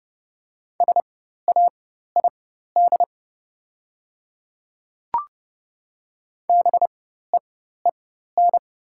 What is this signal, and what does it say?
Morse code, heard as a keyed tone, about 30 wpm, spaced as for 14 wpm, 700 Hz. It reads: SAID BEEN
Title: 4x Character Spacing - Top 100 Words - 30wpm